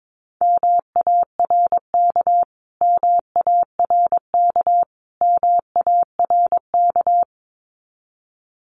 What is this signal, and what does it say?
Morse code, heard as a keyed tone, about 22 wpm, 700 Hz. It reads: MARK MARK MARK
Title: Top 500 Words Repeated 22wpm